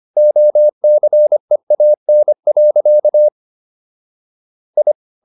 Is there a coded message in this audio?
Yes: Morse code from 0.2 to 4.9 s, heard as a keyed tone, about 25 wpm, 600 Hz. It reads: OCEAN. I